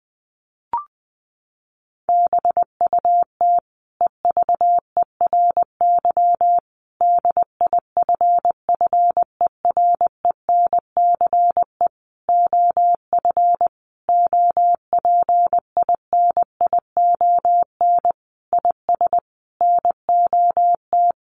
Morse code, heard as a keyed tone, about 20 wpm, 700 Hz. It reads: BUT EVERY DIFFERENCE OF OPINION IS NOT
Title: Sentences from Top 1400 Words 20wpm